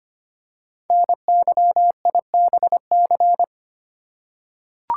Morse code, heard as a keyed tone, about 25 wpm, 700 Hz. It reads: NYIBC
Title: Code Groups - 5 Letter:Numbers - 25wpm